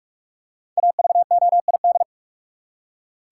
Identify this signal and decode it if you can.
Morse code, heard as a keyed tone, about 46 wpm, 700 Hz. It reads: AVOID
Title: Five Letter Words - 45wpm